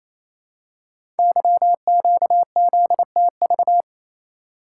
Morse code, heard as a keyed tone, about 28 wpm, 700 Hz. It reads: YQZTV
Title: Code Groups - 5 Letter:Numbers - 28wpm